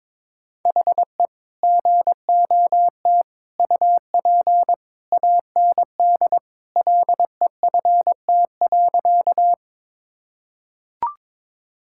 Morse code, heard as a keyed tone, about 22 wpm, 700 Hz. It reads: HE GOT UP AND LEFT.
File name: practice / 22wpm / Sentences with 2-Word Phrases - 22wpm